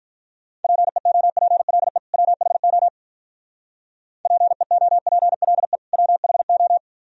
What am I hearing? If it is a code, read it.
Morse code, heard as a keyed tone, about 42 wpm, 700 Hz. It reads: PEOPLE WHO PEOPLE WHO
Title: Speed-Racing - Sets of 2 Words 35wpm